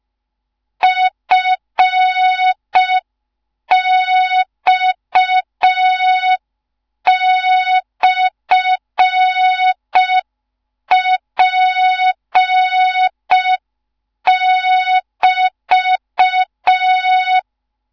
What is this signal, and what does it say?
Morse code, heard as a keyed tone, about 5 wpm, 3700 Hz. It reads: FX/P<BT>
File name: Morze-HA2MN / hang / 15 GYAKORLAT